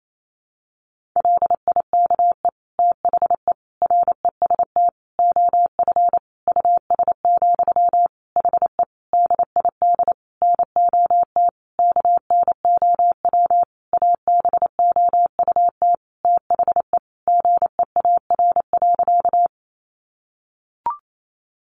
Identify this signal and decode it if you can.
Morse code, heard as a keyed tone, about 28 wpm, 700 Hz. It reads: LIKE THE REST OF US, HE DID NOT KNOW ABOUT THE GEAR.
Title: Sentences from Top 1400 Words 28wpm